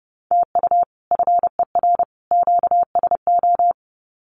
Morse code, heard as a keyed tone, about 30 wpm, 700 Hz. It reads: TU FER QSO
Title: Common QSO Phrases 30wpm